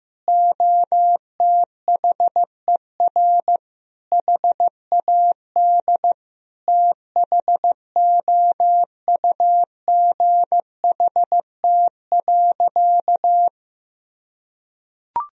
Morse code, heard as a keyed tone, about 16 wpm, 700 Hz. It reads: OTHER HAD THOUGHT.
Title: Sentences from Top 1000 Words - Review Part 1 - 15wpm